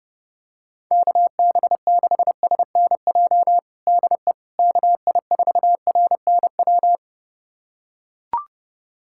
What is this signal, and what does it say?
Morse code, heard as a keyed tone, about 32 wpm, 700 Hz. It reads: KB6SNJ DE KI4RNW
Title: USA Callsign DE Callsign - 6 Characters - 30wpm